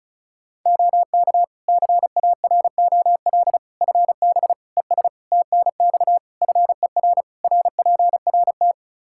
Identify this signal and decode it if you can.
Morse code, heard as a keyed tone, about 34 wpm, 700 Hz. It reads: OK CAROL FB ES TNX FER RPRT
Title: K1USN SST QSO Practice 35wpm